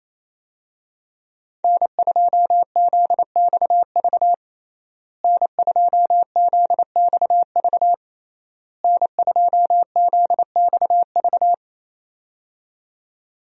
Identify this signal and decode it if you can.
Morse code, heard as a keyed tone, about 27 wpm, 700 Hz. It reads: N2ZXV N2ZXV N2ZXV
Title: USA Call Signs - 5 Characters - Repeated 3x - 28wpm